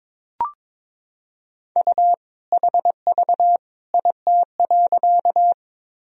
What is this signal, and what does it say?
Morse code, heard as a keyed tone, about 22 wpm, 700 Hz. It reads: U HV IT.
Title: Sentences from Top 100 Words and Abbreviations 22wpm